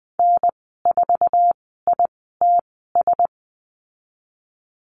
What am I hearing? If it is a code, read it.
Morse code, heard as a keyed tone, about 20 wpm, spaced as for 15 wpm, 700 Hz. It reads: N4ITS
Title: 069 Lesson 12g - Letter d - Call Signs 20wpm